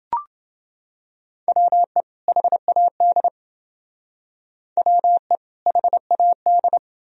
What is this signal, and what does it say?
Morse code, heard as a keyed tone, about 28 wpm, 700 Hz. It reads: WE HAD WE HAD